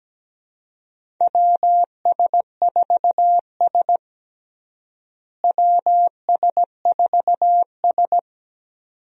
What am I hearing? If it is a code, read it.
Morse code, heard as a keyed tone, about 17 wpm, 700 Hz. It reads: WS4S WS4S
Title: USA Call Signs - 4 Characters - Repeated 2x - 17wpm